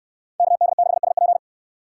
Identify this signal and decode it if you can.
Morse code, heard as a keyed tone, about 65 wpm, 700 Hz. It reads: KD6SP